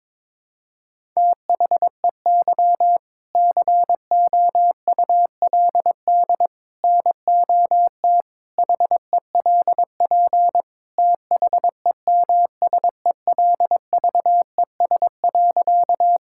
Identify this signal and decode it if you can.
Morse code, heard as a keyed tone, about 22 wpm, 700 Hz. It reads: THEY COULD NOT HELP THEMSELVES.
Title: Sentences from Top 1000 Words - Review Part 5 - 22wpm